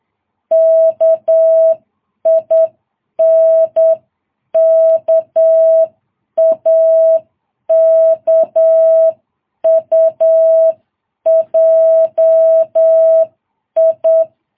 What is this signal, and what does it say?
Morse code, heard as a keyed tone, about 8 wpm, 650 Hz. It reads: KINKAKUJI
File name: jh8cbh